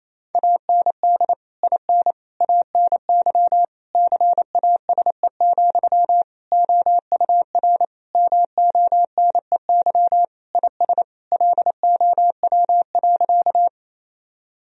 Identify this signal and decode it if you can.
Morse code, heard as a keyed tone, about 28 wpm, 700 Hz. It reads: AND IN ANY CASE, OUR MONEY IS LOW.
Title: Sentences from Top 1000 Words - Review Part 1 - 28wpm